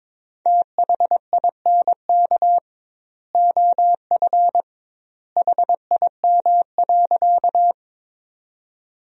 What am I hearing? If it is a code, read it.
Morse code, heard as a keyed tone, about 21 wpm, 700 Hz. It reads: THINK OF HIM.
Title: 2x Word-Spacing - Sentences from Top 200 Words - 22wpm